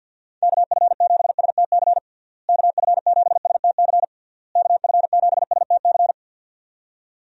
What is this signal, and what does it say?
Morse code, heard as a keyed tone, about 50 wpm, 700 Hz. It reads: KF7STC KF7STC KF7STC
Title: USA Call Signs - 6 Characters - Repeated 3x - 50wpm